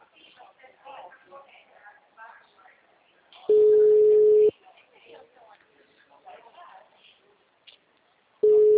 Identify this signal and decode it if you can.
Ringback tone (Europe, 425 Hz), with speech.
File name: out- -2019- - - .94884